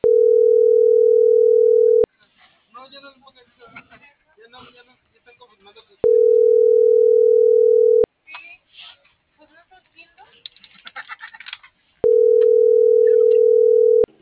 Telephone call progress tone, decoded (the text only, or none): ringback tone